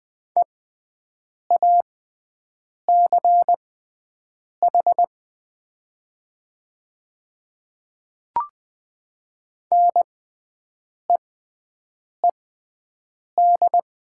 Morse code, heard as a keyed tone, about 20 wpm, spaced as for 7 wpm, 700 Hz. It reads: EACH NEED
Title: 6x Character Spacing - Top 300 Words - 20wpm